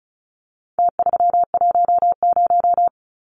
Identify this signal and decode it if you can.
Morse code, heard as a keyed tone, about 34 wpm, 700 Hz. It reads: T310
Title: Numbers - Contest Style - 4-Digit 35wpm